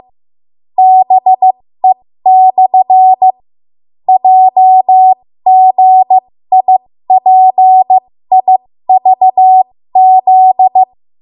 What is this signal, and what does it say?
Morse code, heard as a keyed tone, about 15 wpm, spaced as for 12 wpm, 750 Hz. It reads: BE/JGIPIVZ
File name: Morse / 13WPM